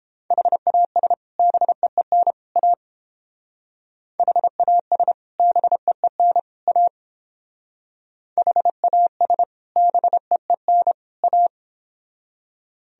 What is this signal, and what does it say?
Morse code, heard as a keyed tone, about 28 wpm, 700 Hz. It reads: HAS BEEN A HAS BEEN A HAS BEEN A